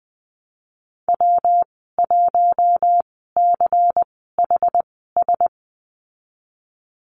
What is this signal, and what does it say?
Morse code, heard as a keyed tone, about 20 wpm, spaced as for 14 wpm, 700 Hz. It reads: W1CHS
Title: 128 Lesson 21g - Number 6 - Call Signs 20wpm